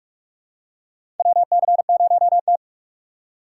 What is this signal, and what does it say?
Morse code, heard as a keyed tone, about 42 wpm, 700 Hz. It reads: WC0T
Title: USA Call Signs - 3x Character Spacing - 45wpm